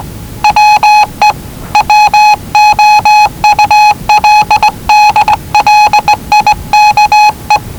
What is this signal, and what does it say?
Morse code, heard as a keyed tone, about 19 wpm, 840 Hz. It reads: WE WOULD LIKE
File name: WellingtonRadioZlwCloseDownOnHfAndMf / 01-Zlw 4277khz 417.5 qrt until 0925z